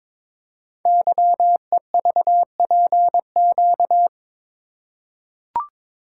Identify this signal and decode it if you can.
Morse code, heard as a keyed tone, about 22 wpm, 700 Hz. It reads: YEVPQ
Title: Code Groups - 5 Letters - 22wpm